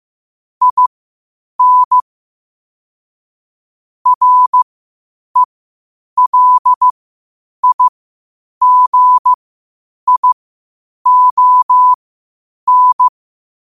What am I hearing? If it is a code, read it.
Morse code, heard as a keyed tone, about 15 wpm, spaced as for 8 wpm, 1000 Hz. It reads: IN RELIGION
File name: qotdfiles / 8wpm